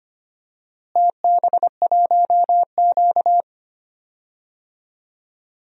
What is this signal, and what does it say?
Morse code, heard as a keyed tone, about 25 wpm, 700 Hz. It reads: TB1Q
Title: DXCC-100 Call Signs - 4 Characters - Repeated 2x - 25wpm